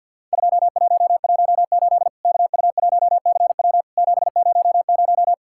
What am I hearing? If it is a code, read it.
Morse code, heard as a keyed tone, about 51 wpm, 700 Hz. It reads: 2118 KU1CW 700
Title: Worked All Europe DX Contest - QTC Exchange with Repeats - 50wpm